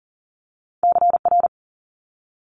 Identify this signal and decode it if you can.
Morse code, heard as a keyed tone, about 40 wpm, 700 Hz. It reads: CR
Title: Multi-letter Phonemes Quick Fire 40wpm